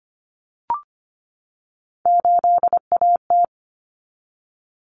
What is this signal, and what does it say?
Morse code, heard as a keyed tone, about 25 wpm, 700 Hz. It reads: 8AT
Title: Common Cut Numbers - 3-Digit 25wpm